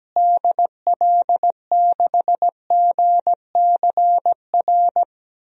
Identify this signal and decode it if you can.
Morse code, heard as a keyed tone, about 17 wpm, 700 Hz. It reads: DL6GCR
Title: DXCC-100 Call Signs - 6 Characters - Repeated 3x - 17wpm